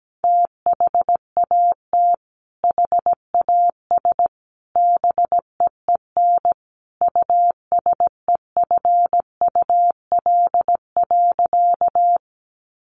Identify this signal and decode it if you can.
Morse code, heard as a keyed tone, about 17 wpm, 700 Hz. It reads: THAT HAS BEEN USEFUL.
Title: Sentences from Top 700 Words 17wpm